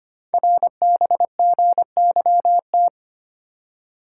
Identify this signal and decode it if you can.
Morse code, heard as a keyed tone, about 25 wpm, 700 Hz. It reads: RBGYT
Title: Code Groups - 5 Letters - 25wpm